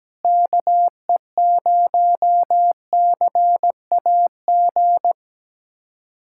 Morse code, heard as a keyed tone, about 17 wpm, 700 Hz. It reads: KE0CAG
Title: USA Callsign DE Callsign 17wpm